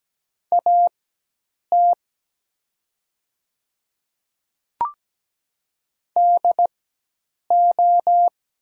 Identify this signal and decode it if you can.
Morse code, heard as a keyed tone, about 17 wpm, spaced as for 8 wpm, 700 Hz. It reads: AT DO
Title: 4x Character Spacing - Top 100 Words - 17wpm